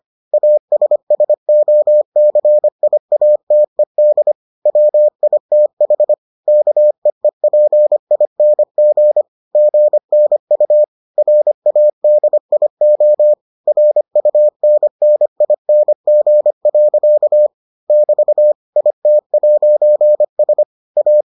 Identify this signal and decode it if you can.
Morse code, heard as a keyed tone, about 25 wpm, 600 Hz. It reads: ASSOCIATED WITH KEEPING GNU RADIO RUNNING. <BT> IT'S A